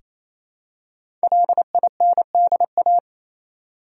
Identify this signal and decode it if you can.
Morse code, heard as a keyed tone, about 28 wpm, 700 Hz. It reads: LINDA
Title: USA Common Names - Max 5 Letters - 28wpm